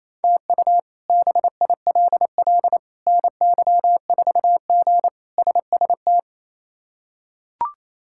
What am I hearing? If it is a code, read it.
Morse code, heard as a keyed tone, about 28 wpm, 700 Hz. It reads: TU BILL NY4G SST
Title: K1USN SST QSO Practice - 28wpm